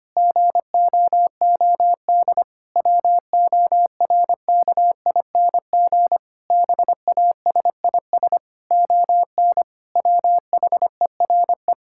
Morse code, heard as a keyed tone, about 25 wpm, 700 Hz. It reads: GOOD WORKING BASIS ON WHERE
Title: Sentences from Top 1000 Words - Review Part 5 - 25wpm